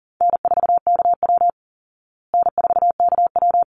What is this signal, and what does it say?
Morse code, heard as a keyed tone, about 37 wpm, 700 Hz. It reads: N4KW N4KW
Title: Call Signs Repeated 3x - 40wpm